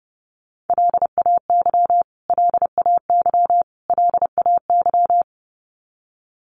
Morse code, heard as a keyed tone, about 30 wpm, 700 Hz. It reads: LAY LAY LAY
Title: Top 500 Words Repeated 30wpm